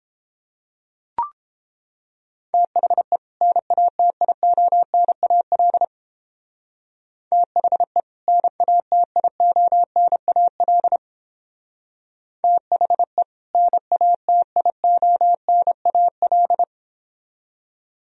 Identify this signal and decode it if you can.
Morse code, heard as a keyed tone, about 29 wpm, 700 Hz. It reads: THE NATIONAL THE NATIONAL THE NATIONAL